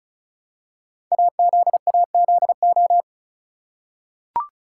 Morse code, heard as a keyed tone, about 35 wpm, 700 Hz. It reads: AZAZO
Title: Code Groups - 5 Letters - 35wpm